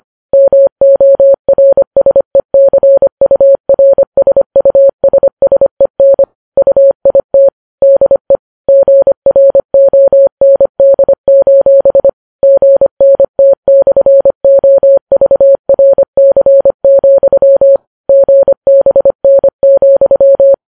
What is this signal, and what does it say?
Morse code, heard as a keyed tone, about 25 wpm, 550 Hz. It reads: MORSECURSUSSEN UIT DE GROND: GNT/OVRC, GBN,